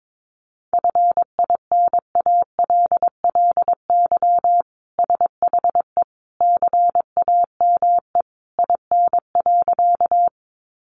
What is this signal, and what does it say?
Morse code, heard as a keyed tone, about 22 wpm, 700 Hz. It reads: FINALLY SHE CAME IN.
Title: Sentences from Top 1000 Words 22wpm